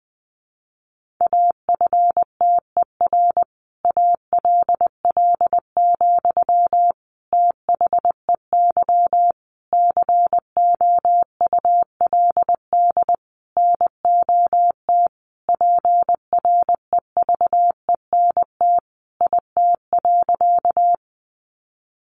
Morse code, heard as a keyed tone, about 20 wpm, 700 Hz. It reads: AFTER ALL, THEY COULD NOT PREVENT IT.
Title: Sentences from Top 1100 Words 20wpm